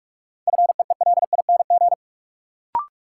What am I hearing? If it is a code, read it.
Morse code, heard as a keyed tone, about 46 wpm, 700 Hz. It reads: FEELING
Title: Sentences from Top 1200 Words - New Word Drill Alt - 45wpm